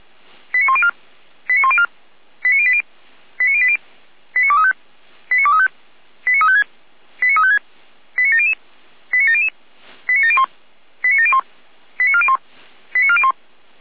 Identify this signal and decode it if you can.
Five-tone selective call (ZVEI), heard as quick runs of five tones, tones of about 70 ms. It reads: 89104 89104 89080 89080 89235 89235 89356 89356 89700 89700 89701 89701 89401 89401
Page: SDR/ZVEI